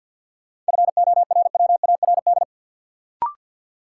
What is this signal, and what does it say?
Morse code, heard as a keyed tone, about 52 wpm, 700 Hz. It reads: FORWARD
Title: Sentences from Top 1300 Words - New Word Drill - 50wpm